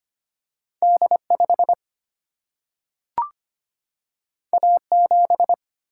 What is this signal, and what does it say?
Morse code, heard as a keyed tone, about 25 wpm, 700 Hz. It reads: D 5 A 7
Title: Single-Letter Single-Number 25wpm